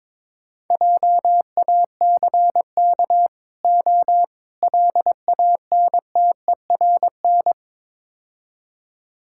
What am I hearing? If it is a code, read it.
Morse code, heard as a keyed tone, about 22 wpm, 700 Hz. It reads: JACK O LANTERN
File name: Easy Words - Speed-Racing with Hints - 22wpm